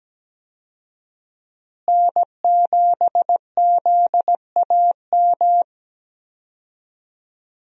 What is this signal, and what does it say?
Morse code, heard as a keyed tone, about 17 wpm, 700 Hz. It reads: N7ZAM
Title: USA Call Signs - 5 Characters - Repeated 3x - 17wpm